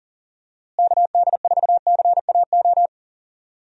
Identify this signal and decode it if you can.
Morse code, heard as a keyed tone, about 37 wpm, 700 Hz. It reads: KD4CAO
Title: USA Call Signs - 6 Characters - Repeated 2x - 40wpm